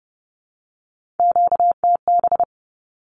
Morse code, heard as a keyed tone, about 30 wpm, 700 Hz. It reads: QTB
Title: Q-Codes Repeated 3x 30wpm